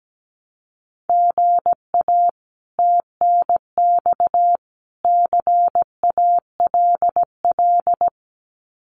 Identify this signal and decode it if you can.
Morse code, heard as a keyed tone, about 17 wpm, 700 Hz. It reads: GA TNX CALL
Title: Common QSO Phrases 17wpm